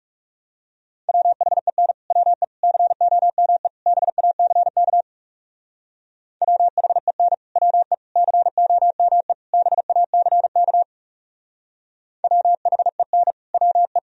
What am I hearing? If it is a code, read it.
Morse code, heard as a keyed tone, about 36 wpm, 700 Hz. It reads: WHEN WE COME BACK WHEN WE COME BACK WHEN WE
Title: Speed-Racing - Sets of 4 Words 30wpm